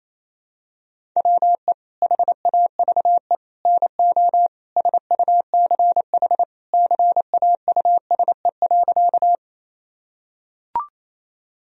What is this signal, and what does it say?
Morse code, heard as a keyed tone, about 28 wpm, 700 Hz. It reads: WE HAVE NO SUCH CAUSE.
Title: Sentences from Top 1000 Words - Review Part 1 - 28wpm